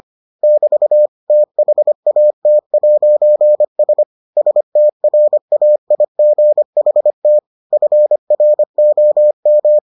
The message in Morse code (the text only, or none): <BT> THAT'S STRAIGHT FROM